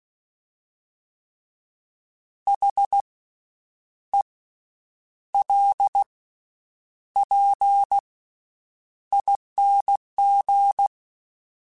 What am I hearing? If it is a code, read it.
Morse code, heard as a keyed tone, about 16 wpm, spaced as for 8 wpm, 800 Hz. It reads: HELPING
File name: Special INGs / Special INGs-mp3s / VerySlowToFaster Ing @15